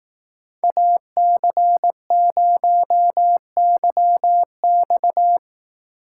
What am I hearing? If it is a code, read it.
Morse code, heard as a keyed tone, about 18 wpm, 700 Hz. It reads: AC0YX
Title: Speed Racing - USA 2x2 Call Signs 15wpm